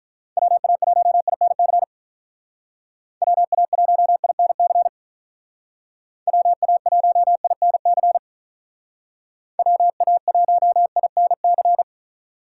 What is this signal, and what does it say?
Morse code, heard as a keyed tone, about 43 wpm, 700 Hz. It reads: WA1INC WA1INC WA1INC WA1INC
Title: Speed Racing - USA 2x3 Call Signs 35wpm